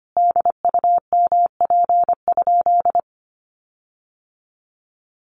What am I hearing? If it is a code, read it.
Morse code, heard as a keyed tone, about 25 wpm, 700 Hz. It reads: DUMP?
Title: Sentences from Top 1300 Words 25wpm